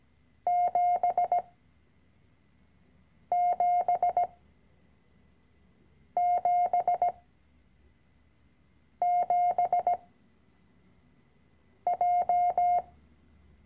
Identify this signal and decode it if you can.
Morse code, heard as a keyed tone, about 17 wpm, spaced as for 4 wpm, 710 Hz. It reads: 7777J
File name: CW-Online-Kurs / Anfänger / Lektion 010